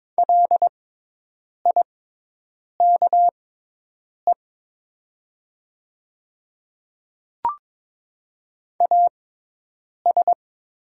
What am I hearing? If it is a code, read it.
Morse code, heard as a keyed tone, about 23 wpm, spaced as for 8 wpm, 690 Hz. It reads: LIKE AS
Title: 6x Character Spacing - Top 200 Words - 22wpm